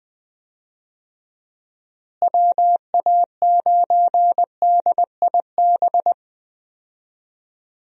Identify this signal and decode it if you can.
Morse code, heard as a keyed tone, about 20 wpm, 700 Hz. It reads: WA9DIB
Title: USA Call Signs - 6 Characters - Repeated 2x - 20wpm